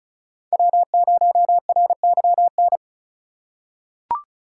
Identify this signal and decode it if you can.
Morse code, heard as a keyed tone, about 34 wpm, 700 Hz. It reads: W0RYN
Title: USA Call Signs - 5 Characters - 4x Character Spacing - 35wpm